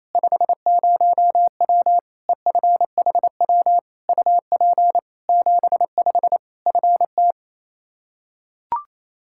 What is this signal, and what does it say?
Morse code, heard as a keyed tone, about 28 wpm, 700 Hz. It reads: 50W EFHW UP 75 FT
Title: Practice QSO - Exchange 2 - Info - 28wpm